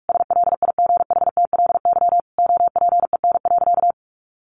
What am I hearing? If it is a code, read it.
Morse code, heard as a keyed tone, about 44 wpm, 700 Hz. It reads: SLIGHTLY OPEN.
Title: Sentences from Top 1300 Words 45wpm